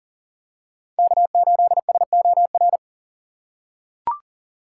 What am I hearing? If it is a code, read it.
Morse code, heard as a keyed tone, about 40 wpm, 700 Hz. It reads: K8SOR
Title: USA Call Signs - 5 Characters - 3x Character Spacing - 40wpm